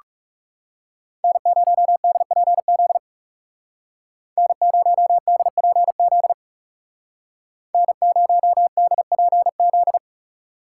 Morse code, heard as a keyed tone, about 37 wpm, 700 Hz. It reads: N0DPZ N0DPZ N0DPZ
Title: Speed Racing - USA 1x3 Call Signs 30wpm